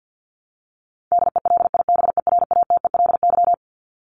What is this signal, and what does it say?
Morse code, heard as a keyed tone, about 50 wpm, 700 Hz. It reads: DELIBERATELY
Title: Top 1751-2000 Words 50wpm